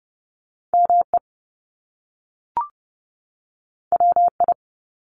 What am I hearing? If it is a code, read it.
Morse code, heard as a keyed tone, about 30 wpm, 700 Hz. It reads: ME WI
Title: US States 30wpm